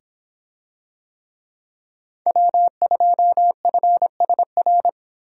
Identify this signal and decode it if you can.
Morse code, heard as a keyed tone, about 25 wpm, 700 Hz. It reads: W2FSR